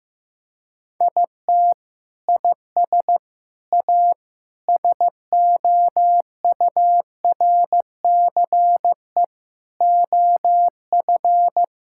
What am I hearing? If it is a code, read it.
Morse code, heard as a keyed tone, about 15 wpm, 700 Hz. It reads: IT IS A SOURCE OF
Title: Sentences from Top 1000 Words - Review Part 5 - 15wpm